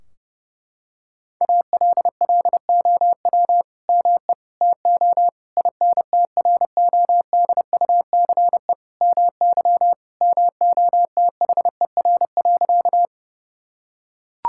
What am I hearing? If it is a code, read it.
Morse code, heard as a keyed tone, about 32 wpm, 700 Hz. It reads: ALLOW ME TO INTRODUCE MY MOTHER.
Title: Sentences from Top 1600 Words 30wpm